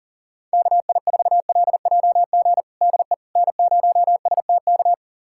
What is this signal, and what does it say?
Morse code, heard as a keyed tone, about 40 wpm, 700 Hz. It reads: KI4LJG DE N0STK
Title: USA Callsign DE Callsign 40wpm